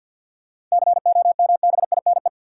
Morse code, heard as a keyed tone, about 49 wpm, 700 Hz. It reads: COMBINE
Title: Sentences from Top 1600 Words - New Word Drill - 50wpm